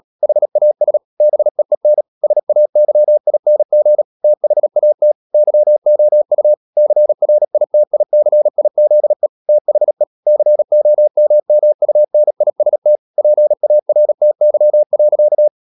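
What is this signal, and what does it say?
Morse code, heard as a keyed tone, about 37 wpm, 600 Hz. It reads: HAS BEEN SAYING THAT YOU CRITICIZE THE COMMUNIST PARTY.
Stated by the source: Morse practice at 36 wpm, text of Ken Follett's novel Never, from CW med Gnister.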